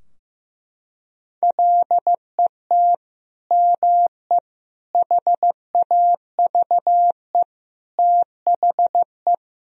Morse code, heard as a keyed tone, about 16 wpm, 700 Hz. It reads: LET ME HAVE THE
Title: Sentences from Top 1000 Words - Review Part 2 - 15wpm